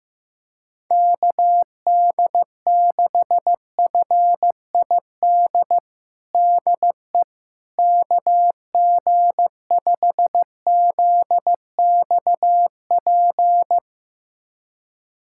Morse code, heard as a keyed tone, about 15 wpm, 700 Hz. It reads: KD6FID DE KG5ZXP
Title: USA Callsign DE Callsign - 6 Characters - 15wpm